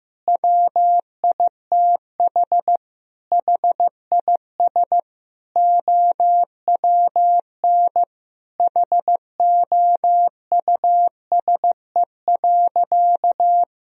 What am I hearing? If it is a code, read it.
Morse code, heard as a keyed tone, about 16 wpm, 700 Hz. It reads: WITH HIS OWN HOUSE.
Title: Sentences from Top 1000 Words - Review Part 1 - 15wpm